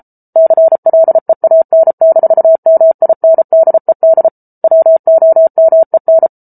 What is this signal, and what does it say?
Morse code, heard as a keyed tone, about 34 wpm, 650 Hz. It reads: CLEAN-MINDED WOMEN